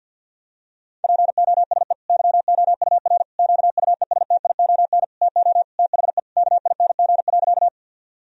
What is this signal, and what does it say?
Morse code, heard as a keyed tone, about 50 wpm, 700 Hz. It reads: POSE YOUR QUESTION TO THE KING.
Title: Sentences from Top 1000 Words - Review Part 2 - 50wpm